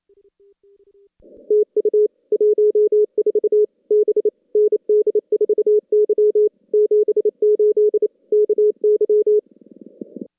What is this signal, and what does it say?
Morse code, heard as a keyed tone, about 30 wpm, 410 Hz. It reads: TU 14 B ND4Y 78 KY
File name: contests / CW-SS-k5tr- -2100